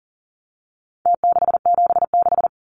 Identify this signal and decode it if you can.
Morse code, heard as a keyed tone, about 40 wpm, 700 Hz. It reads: T676
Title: Numbers - Contest Style - 4-Digit 40wpm